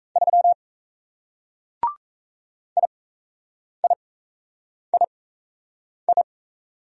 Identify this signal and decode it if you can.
Morse code, heard as a keyed tone, about 37 wpm, spaced as for 8 wpm, 700 Hz. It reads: 3 IIII